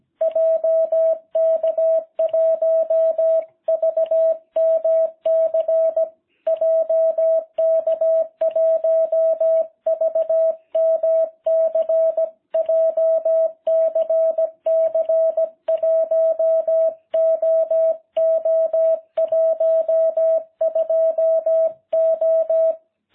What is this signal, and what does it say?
Morse code, heard as a keyed tone, about 17 wpm, 660 Hz. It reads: JK1VMC JK1VMC JCC 1OO12O